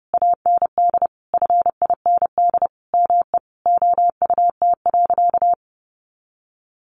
Morse code, heard as a keyed tone, about 30 wpm, 700 Hz. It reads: AND FIND ME OUT.
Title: Sentences from Top 500 Words 30wpm